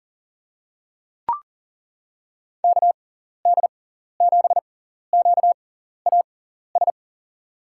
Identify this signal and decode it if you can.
Morse code, heard as a keyed tone, about 40 wpm, spaced as for 14 wpm, 700 Hz. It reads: KD7QAS
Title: USA Call Signs - 6 Characters - 6x Character Spacing - 40wpm